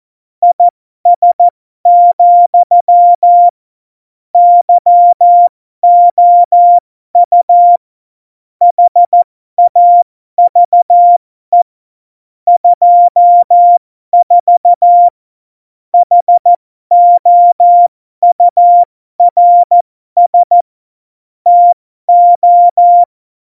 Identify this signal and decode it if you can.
Morse code, heard as a keyed tone, about 14 wpm, spaced as for 12 wpm, 700 Hz. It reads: IS, YOU HAVE 24 HOURS TO
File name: Morsekurs / Witze